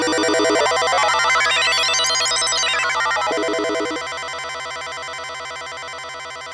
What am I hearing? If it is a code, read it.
There is no code.